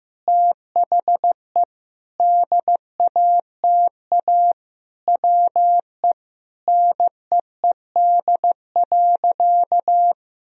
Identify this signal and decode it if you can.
Morse code, heard as a keyed tone, about 15 wpm, 700 Hz. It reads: THE DATA WE NEED.
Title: Sentences from Top 1000 Words - Review Part 3 - 15wpm